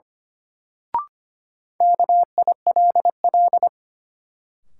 Morse code, heard as a keyed tone, about 25 wpm, 700 Hz. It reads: KILL
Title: Sentences from Top 1400 Words - New Word Drill - 25wpm